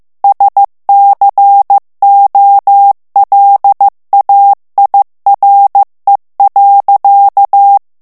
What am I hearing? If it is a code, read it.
Morse code, heard as a keyed tone, about 15 wpm, 800 Hz. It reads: SCOLAIRE.
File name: morse mp3 / FRANCAIS TEXTE CLAIR 1 VITESSE 14 MOTS MINUTES